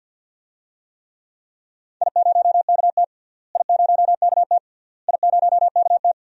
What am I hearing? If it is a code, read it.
Morse code, heard as a keyed tone, about 46 wpm, 700 Hz. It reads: I0KT I0KT I0KT
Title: DXCC-100 Call Signs - 4 Characters - Repeated 3x - 50wpm